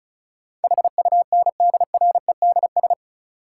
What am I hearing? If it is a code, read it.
Morse code, heard as a keyed tone, about 36 wpm, 700 Hz. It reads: HUNDREDS
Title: Sets of 4 Words - New Word Drill - 35wpm